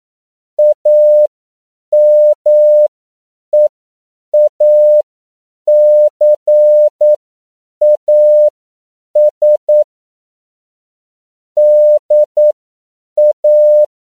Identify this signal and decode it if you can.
Morse code, heard as a keyed tone, about 9 wpm, spaced as for 7 wpm, 600 Hz. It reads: AMEACAS DA
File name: CW / 10PPM